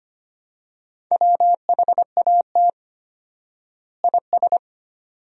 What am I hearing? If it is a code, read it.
Morse code, heard as a keyed tone, about 25 wpm, 700 Hz. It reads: WHAT IS
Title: 4x Word-Spacing - Sentences from Top 300 Words - 25wpm